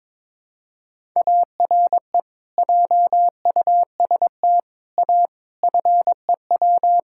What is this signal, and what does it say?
Morse code, heard as a keyed tone, about 22 wpm, 700 Hz. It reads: ARE JUST A FEW
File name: practice / 22wpm / Sets of 6 Words - 22wpm